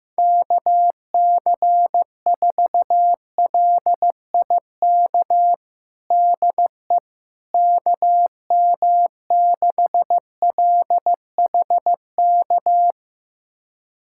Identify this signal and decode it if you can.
Morse code, heard as a keyed tone, about 15 wpm, 700 Hz. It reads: KC4LIK DE KM6LHK
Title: USA Callsign DE Callsign - 6 Characters - 15wpm